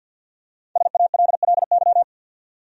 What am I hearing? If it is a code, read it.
Morse code, heard as a keyed tone, about 49 wpm, 700 Hz. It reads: SALLY
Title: USA Common Names - Max 5 Letters - 50wpm